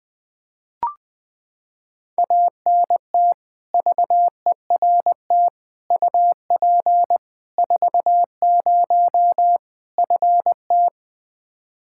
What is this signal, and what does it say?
Morse code, heard as a keyed tone, about 20 wpm, 700 Hz. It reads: ANT VERT UP 40 FT
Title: Practice QSO - Exchange 2 - Info - 20wpm